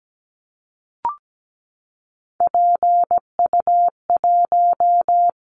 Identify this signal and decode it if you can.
Morse code, heard as a keyed tone, about 17 wpm, 700 Hz. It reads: PU1
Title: Callsign Prefixes for Top 100 DXCC - 17wpm